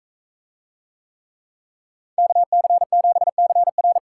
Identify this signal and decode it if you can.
Morse code, heard as a keyed tone, about 44 wpm, 700 Hz. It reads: KC7CR